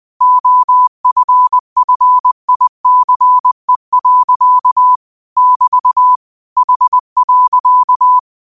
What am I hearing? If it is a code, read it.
Morse code, heard as a keyed tone, about 20 wpm, 1000 Hz. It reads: OFFICE. <BT> H.
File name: qotdfiles / 20wpm